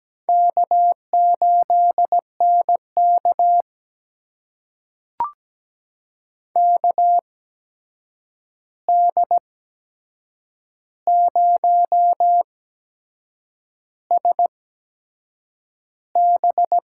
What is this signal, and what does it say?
Morse code, heard as a keyed tone, about 17 wpm, spaced as for 7 wpm, 700 Hz. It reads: K8NK KD0SB
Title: USA Call Signs - 8x Character Spacing - 17wpm